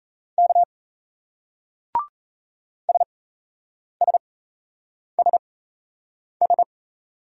Morse code, heard as a keyed tone, about 37 wpm, spaced as for 8 wpm, 700 Hz. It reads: K SSSS